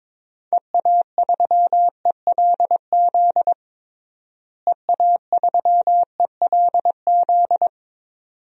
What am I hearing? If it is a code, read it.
Morse code, heard as a keyed tone, about 22 wpm, 700 Hz. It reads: EA3ELZ EA3ELZ
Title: DXCC-100 Call Signs - 6 Characters - Repeated 2x - 22wpm